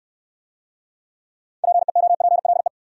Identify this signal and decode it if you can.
Morse code, heard as a keyed tone, about 63 wpm, 700 Hz. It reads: PEOPLE